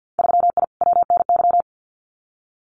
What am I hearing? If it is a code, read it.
Morse code, heard as a keyed tone, about 50 wpm, 700 Hz. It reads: 3I WNY
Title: Winter Field Day Exchanges Encore 50wpm